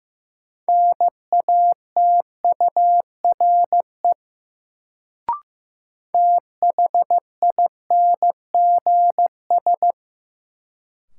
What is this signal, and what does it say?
Morse code, heard as a keyed tone, about 15 wpm, 700 Hz. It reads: NATURE THINGS
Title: Sets of 3 Words - New Word Drill - 15wpm